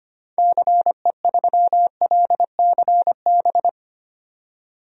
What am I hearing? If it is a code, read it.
Morse code, heard as a keyed tone, about 25 wpm, 700 Hz. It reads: CE3LCB
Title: DXCC-100 Call Signs - 6 Characters - 25wpm